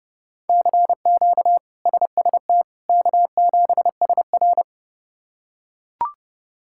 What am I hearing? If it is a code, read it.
Morse code, heard as a keyed tone, about 29 wpm, 700 Hz. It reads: CQ SST K7SR
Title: K1USN SST QSO Practice - 30wpm